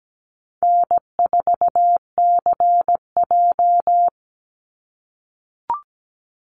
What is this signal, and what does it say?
Morse code, heard as a keyed tone, about 17 wpm, 700 Hz. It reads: N4CJ
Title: Call Signs - 17wpm